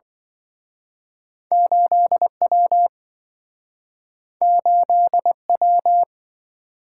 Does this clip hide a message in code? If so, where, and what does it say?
Yes: Morse code from 1.5 to 6.0 s, heard as a keyed tone, about 27 wpm, 700 Hz. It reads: 8W 8W